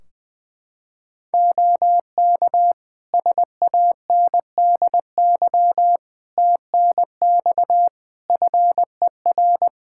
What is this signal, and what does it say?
Morse code, heard as a keyed tone, about 21 wpm, 700 Hz. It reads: OK SANDY TNX FER
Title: K1USN SST QSO Practice 20wpm